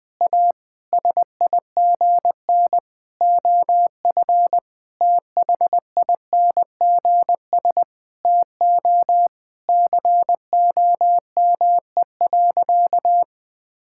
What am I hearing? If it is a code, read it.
Morse code, heard as a keyed tone, about 20 wpm, 700 Hz. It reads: A SIGN OF THINGS TO COME.
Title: Sentences with 6-Word Phrases - Spoken Beginning - 20wpm